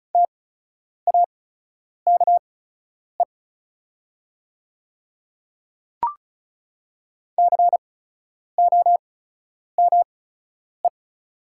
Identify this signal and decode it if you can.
Morse code, heard as a keyed tone, about 36 wpm, spaced as for 10 wpm, 700 Hz. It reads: TAKE COME
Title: 8x Character Spacing - Top 200 Words - 35wpm